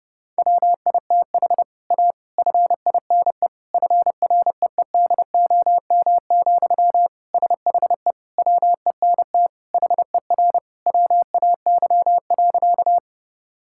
Morse code, heard as a keyed tone, about 32 wpm, 700 Hz. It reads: WITH A FINE FREEDOM, SHE WENT HER WAY.
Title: Sentences from Top 1000 Words - Review Part 5 - 30wpm